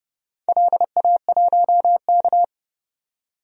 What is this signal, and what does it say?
Morse code, heard as a keyed tone, about 30 wpm, 700 Hz. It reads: LA1K
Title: DXCC-100 Call Signs - 4 Characters - 30wpm